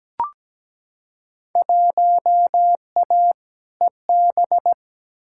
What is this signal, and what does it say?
Morse code, heard as a keyed tone, about 17 wpm, 700 Hz. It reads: 1A EB
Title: Field Day Exchanges - 17wpm